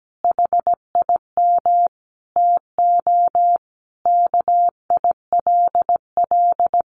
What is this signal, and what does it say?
Morse code, heard as a keyed tone, about 17 wpm, 700 Hz. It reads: HIM TO KILL
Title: Sentences from Top 1400 Words 17wpm